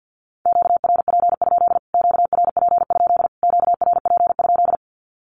Morse code, heard as a keyed tone, about 50 wpm, 700 Hz. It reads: QRP? QRP? QRP?
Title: Q-Codes Repeated 3x 50wpm